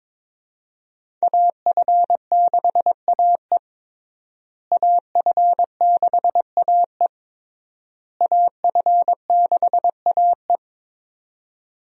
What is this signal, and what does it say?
Morse code, heard as a keyed tone, about 22 wpm, 700 Hz. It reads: AF6AE AF6AE AF6AE
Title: USA Call Signs - 5 Characters - Repeated 3x - 22wpm